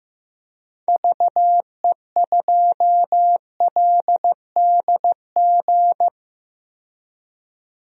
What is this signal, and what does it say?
Morse code, heard as a keyed tone, about 15 wpm, 700 Hz. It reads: VE2LDG
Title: DXCC-100 Call Signs - 6 Characters - 15wpm